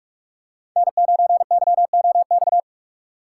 Morse code, heard as a keyed tone, about 46 wpm, 700 Hz. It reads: N9YOX
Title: USA Call Signs - 5 Characters - 4x Character Spacing - 45wpm